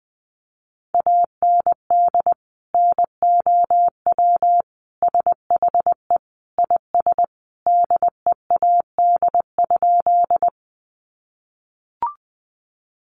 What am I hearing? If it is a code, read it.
Morse code, heard as a keyed tone, about 19 wpm, 700 Hz. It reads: AND NOW SHE IS DEAD?
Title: Sentences from Top 1500 Words 20wpm